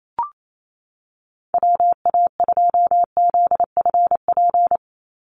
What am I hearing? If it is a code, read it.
Morse code, heard as a keyed tone, about 28 wpm, 700 Hz. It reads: WA2ZFP
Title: USA Call Signs 28wpm